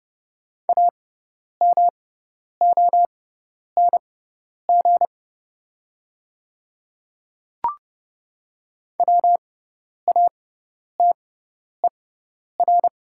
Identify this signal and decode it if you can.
Morse code, heard as a keyed tone, about 31 wpm, spaced as for 10 wpm, 700 Hz. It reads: AMONG WATER
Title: 6x Character Spacing - Top 500 Words - 30wpm